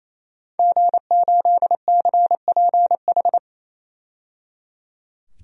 Morse code, heard as a keyed tone, about 28 wpm, 700 Hz. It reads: G8CPH
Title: DXCC-100 Call Signs - 5 Characters - 28wpm